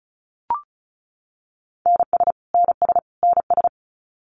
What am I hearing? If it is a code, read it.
Morse code, heard as a keyed tone, about 34 wpm, 700 Hz. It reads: NS NS NS
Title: Canadian Provinces and Territories - Repeated 3x - 35wpm